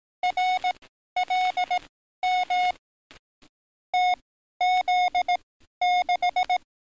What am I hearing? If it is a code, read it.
Morse code, heard as a keyed tone, about 18 wpm, spaced as for 12 wpm, 730 Hz. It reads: RLM TZ6
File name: training / NAVPERS 11049to11060 / NAVPERS 11052 RA RB